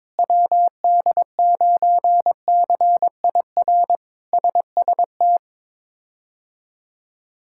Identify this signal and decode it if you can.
Morse code, heard as a keyed tone, about 22 wpm, 700 Hz. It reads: WD9CIR SST
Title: K1USN SST QSO Practice - 22wpm